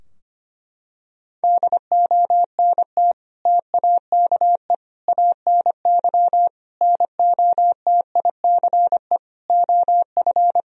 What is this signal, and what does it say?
Morse code, heard as a keyed tone, about 25 wpm, 700 Hz. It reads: DONT TAKE ANY NOTICE OF
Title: Sentences from Top 1000 Words - Review Part 2 - 25wpm